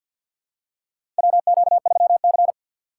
Morse code, heard as a keyed tone, about 48 wpm, 700 Hz. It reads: WQ3C
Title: USA Call Signs - 4 Characters - 2x Character Spacing - 50wpm